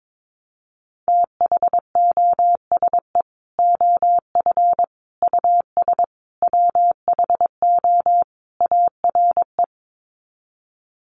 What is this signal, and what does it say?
Morse code, heard as a keyed tone, about 22 wpm, 700 Hz. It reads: THOSE OF US WHO ARE
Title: Sets of 5 Words Encore2 22wpm